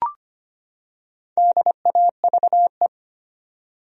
Morse code, heard as a keyed tone, about 25 wpm, 700 Hz. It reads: DAVE
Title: USA Common Names - Max 4 Letters - 25wpm